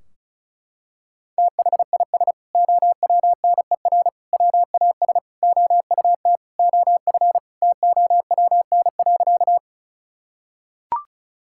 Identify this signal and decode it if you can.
Morse code, heard as a keyed tone, about 36 wpm, 700 Hz. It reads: THIS OWNER WAS OUT OF TOWN.
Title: Sentences from Top 1600 Words 35wpm